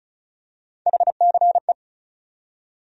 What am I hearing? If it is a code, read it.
Morse code, heard as a keyed tone, about 36 wpm, 700 Hz. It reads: HCE
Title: USA Call Sign Postfixes - 3 Letters 35wpm